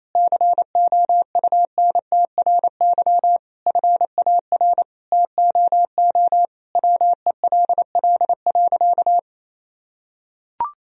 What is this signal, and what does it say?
Morse code, heard as a keyed tone, about 28 wpm, 700 Hz. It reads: COUNTRY FAR TOO WELL.
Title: Sentences from Top 1000 Words - Review Part 3 - 28wpm